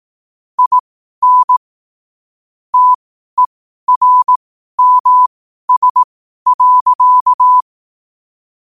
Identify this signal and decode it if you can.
Morse code, heard as a keyed tone, about 18 wpm, spaced as for 12 wpm, 1000 Hz. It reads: IN TERMS.
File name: qotdfiles / 12wpm